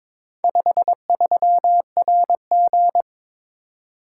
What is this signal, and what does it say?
Morse code, heard as a keyed tone, about 22 wpm, 700 Hz. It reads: 53RG
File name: practice / 22wpm / Four Letter-Number Combinations - 22wpm